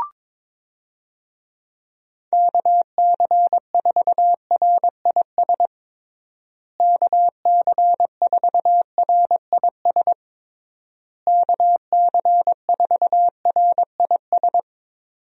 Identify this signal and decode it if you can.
Morse code, heard as a keyed tone, about 22 wpm, 700 Hz. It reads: KC4RIS KC4RIS KC4RIS
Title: USA Call Signs - 6 Characters - Repeated 3x - 22wpm